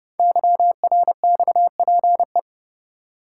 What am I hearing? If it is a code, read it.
Morse code, heard as a keyed tone, about 30 wpm, 700 Hz. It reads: YRXPE
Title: Code Groups - 5 Letters - 30wpm